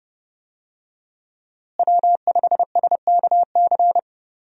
Morse code, heard as a keyed tone, about 30 wpm, 700 Hz. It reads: W5SKC
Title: USA Call Signs - 5 Characters - Repeated 2x - 30wpm